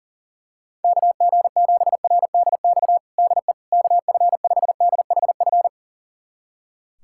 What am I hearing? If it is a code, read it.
Morse code, heard as a keyed tone, about 37 wpm, 700 Hz. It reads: KG7RDX DE KF5DHF
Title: USA Callsign DE Callsign - 6 Characters - 40wpm